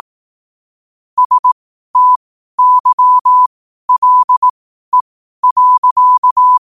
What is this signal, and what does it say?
Morse code, heard as a keyed tone, about 18 wpm, spaced as for 12 wpm, 1000 Hz. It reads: STYLE.
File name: qotdfiles / 12wpm